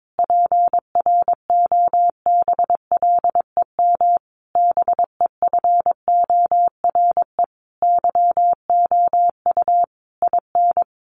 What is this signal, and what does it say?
Morse code, heard as a keyed tone, about 22 wpm, 700 Hz. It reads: PROBLEM BEFORE YOU IN
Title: Sentences from Top 1500 Words 22wpm